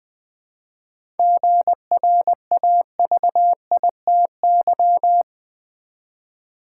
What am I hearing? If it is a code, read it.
Morse code, heard as a keyed tone, about 20 wpm, 700 Hz. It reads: GRAVITY
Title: 7 Letter Words - First 3 Letters Given - 20wpm